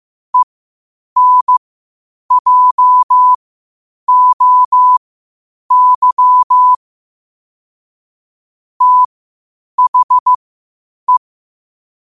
Morse code, heard as a keyed tone, about 15 wpm, spaced as for 8 wpm, 1000 Hz. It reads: ENJOY THE
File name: qotdfiles / 8wpm